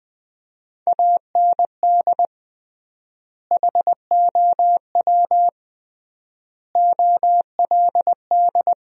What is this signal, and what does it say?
Morse code, heard as a keyed tone, about 20 wpm, 700 Hz. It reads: AND HOW OLD
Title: 3x Word-Spacing - Sentences from Top 200 Words - 20wpm